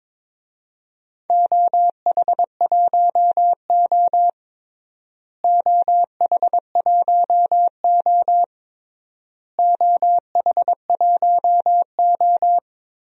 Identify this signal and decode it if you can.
Morse code, heard as a keyed tone, about 22 wpm, 700 Hz. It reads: OH1O OH1O OH1O
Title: DXCC-100 Call Signs - 4 Characters - Repeated 3x - 22wpm